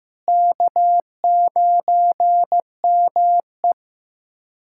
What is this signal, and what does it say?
Morse code, heard as a keyed tone, about 15 wpm, 700 Hz. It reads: K9ME
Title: USA Callsign DE Callsign - 4 Characters - 15wpm